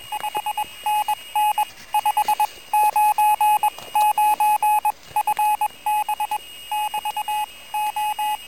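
Morse code, heard as a keyed tone, about 21 wpm, spaced as for 18 wpm, 850 Hz. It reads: 5NN 599 FB <BT> O
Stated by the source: QRP contact between UA1CEG and UA3YLP using homemade gear on 14060 kHz during a QRP round table, with heavy interference from a contest.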